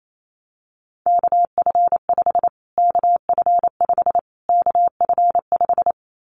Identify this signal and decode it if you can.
Morse code, heard as a keyed tone, about 28 wpm, 700 Hz. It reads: KF5 KF5 KF5
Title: Callsign Prefixes for Top 25 DXCC - Repeated 3x - 28wpm